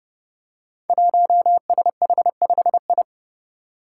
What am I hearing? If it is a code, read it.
Morse code, heard as a keyed tone, about 30 wpm, 700 Hz. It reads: 1SH5I
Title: Five Challenging Letter-Number Combinations 30wpm